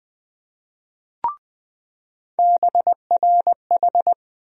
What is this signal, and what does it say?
Morse code, heard as a keyed tone, about 20 wpm, 700 Hz. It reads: BRH
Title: USA Call Sign Postfixes - 3 Letters 20wpm